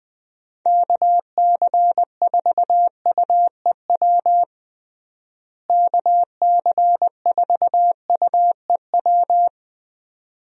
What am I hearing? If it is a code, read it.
Morse code, heard as a keyed tone, about 20 wpm, 700 Hz. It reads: KC4UEW KC4UEW
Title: USA Call Signs - 6 Characters - Repeated 2x - 20wpm